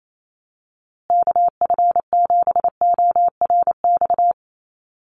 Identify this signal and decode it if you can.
Morse code, heard as a keyed tone, about 28 wpm, 700 Hz. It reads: KF7ORX
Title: USA Call Signs Repeated 3x 28wpm